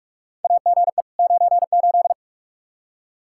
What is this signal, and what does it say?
Morse code, heard as a keyed tone, about 46 wpm, 700 Hz. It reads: AGE 98
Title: Practice QSO - Exchange 3 - More Info - 45wpm